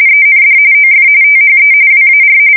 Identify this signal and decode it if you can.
RTTY, about 45.45 baud. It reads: END OF THE WAR
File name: fsk170 / usnavy4